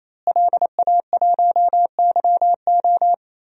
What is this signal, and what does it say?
Morse code, heard as a keyed tone, about 27 wpm, 700 Hz. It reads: LA1YO
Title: DXCC-100 Call Signs - 5 Characters - Repeated 3x - 28wpm